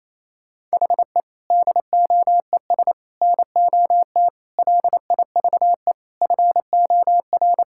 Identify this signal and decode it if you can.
Morse code, heard as a keyed tone, about 28 wpm, 700 Hz. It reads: HE DOES NOT LIVE FOR
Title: Sentences from Top 1600 Words 28wpm